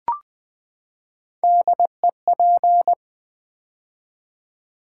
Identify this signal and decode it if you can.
Morse code, heard as a keyed tone, about 20 wpm, 700 Hz. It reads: DEP
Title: Build A Word - 3 Letters At A Time - 6-Letter Words - 20wpm